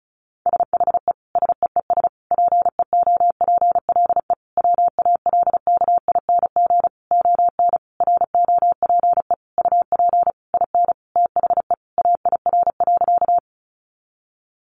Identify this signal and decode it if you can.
Morse code, heard as a keyed tone, about 34 wpm, 700 Hz. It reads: SHE SEES PEOPLE WALKING ON ROPE UP IN THE AIR.
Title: Easy Phrases 35wpm